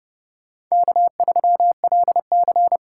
Morse code, heard as a keyed tone, about 38 wpm, 700 Hz. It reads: K3LC